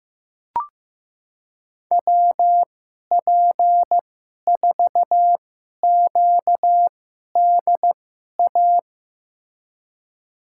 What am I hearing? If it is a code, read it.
Morse code, heard as a keyed tone, about 15 wpm, spaced as for 11 wpm, 700 Hz. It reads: WP4QDA
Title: USA Call Signs - 6 Characters - 2x Character Spacing - 15wpm